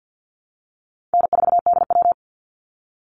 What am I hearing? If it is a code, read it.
Morse code, heard as a keyed tone, about 50 wpm, 700 Hz. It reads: N4DW
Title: Call Signs Repeated 2x - 50wpm